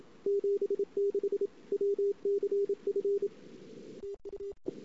Morse code, heard as a keyed tone, about 29 wpm, 410 Hz. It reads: 76 WCF TUE
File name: CW / 2008-ss-cw-k5tr-op-n5rz -22:00